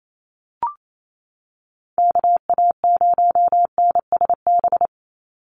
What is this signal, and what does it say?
Morse code, heard as a keyed tone, about 28 wpm, 700 Hz. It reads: KA0NSB
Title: USA Call Signs 28wpm